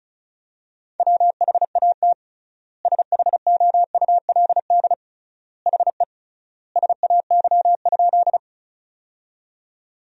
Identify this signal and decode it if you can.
Morse code, heard as a keyed tone, about 36 wpm, 700 Hz. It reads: WHAT SHOULD HE SAY?
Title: 3x Word-Spacing - Sentences from Top 300 Words - 35wpm